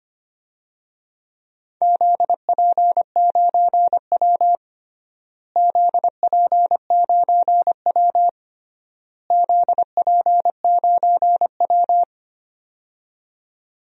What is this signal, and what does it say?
Morse code, heard as a keyed tone, about 25 wpm, 700 Hz. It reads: ZP9W ZP9W ZP9W
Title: DXCC-100 Call Signs - 4 Characters - Repeated 3x - 25wpm